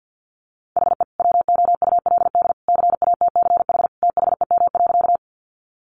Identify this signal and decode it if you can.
Morse code, heard as a keyed tone, about 48 wpm, 700 Hz. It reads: HE WOULD CATCH THEM.
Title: Sentences from Top 1500 Words 50wpm